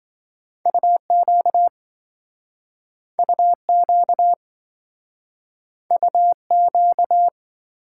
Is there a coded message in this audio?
Yes: Morse code from 0.7 to 7.3 s, heard as a keyed tone, about 27 wpm, 700 Hz. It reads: UQ UQ UQ